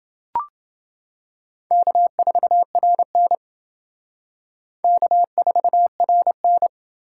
Morse code, heard as a keyed tone, about 28 wpm, 700 Hz. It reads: K4RN K4RN